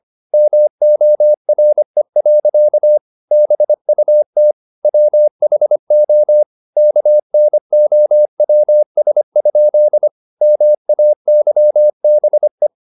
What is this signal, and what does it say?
Morse code, heard as a keyed tone, about 25 wpm, 600 Hz. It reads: MORE. BUT WHO KNOWS? MAYBE
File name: the old man and the sea part 1 25wpm